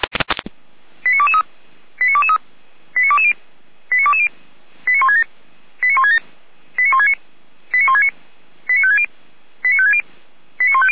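Five-tone selective call (ZVEI), heard as quick runs of five tones, tones of about 70 ms. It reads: 89223 89223 89220 89220 89167 89167 89169 89169 89560 89560 89150
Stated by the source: ZVEI five-tone selective call received on SDR.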